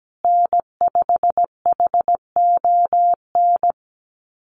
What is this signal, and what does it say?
Morse code, heard as a keyed tone, about 17 wpm, 700 Hz. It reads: N5HON
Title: USA Call Signs Repeated 2x 17wpm